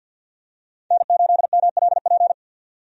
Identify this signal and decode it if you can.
Morse code, heard as a keyed tone, about 52 wpm, 700 Hz. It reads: N8MLP
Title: USA Call Signs - 5 Characters - 50wpm